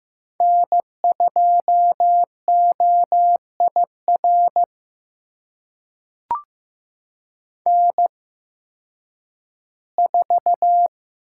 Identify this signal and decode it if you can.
Morse code, heard as a keyed tone, about 15 wpm, spaced as for 10 wpm, 700 Hz. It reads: N2OIR N4
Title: USA Call Signs - 5 Characters - 8x Character Spacing - 15wpm